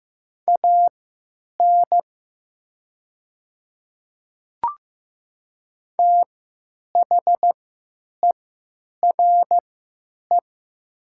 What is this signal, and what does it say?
Morse code, heard as a keyed tone, about 15 wpm, spaced as for 9 wpm, 700 Hz. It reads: AN THERE
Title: 3x Character Spacing - Top 100 Words - 15wpm